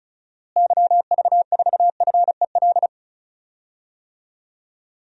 Morse code, heard as a keyed tone, about 36 wpm, 700 Hz. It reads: YV4FEL
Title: DXCC-100 Call Signs - 6 Characters - Repeated 3x - 35wpm